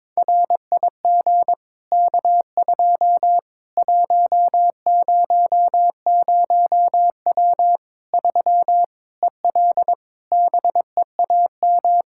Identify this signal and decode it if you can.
Morse code, heard as a keyed tone, about 21 wpm, 700 Hz. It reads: RIG K2 100W 3 EL BEAM
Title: K1USN SST QSO Practice 22wpm